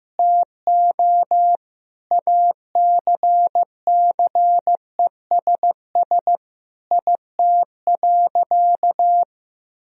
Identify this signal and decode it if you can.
Morse code, heard as a keyed tone, about 15 wpm, 700 Hz. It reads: TO ACCESS IT.
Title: Sentences from Top 1000 Words - Review Part 4 - 15wpm